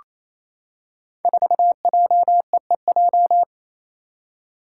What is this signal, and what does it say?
Morse code, heard as a keyed tone, about 28 wpm, 700 Hz. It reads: 4JEEJ
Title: Five Challenging Letter-Number Combinations 28wpm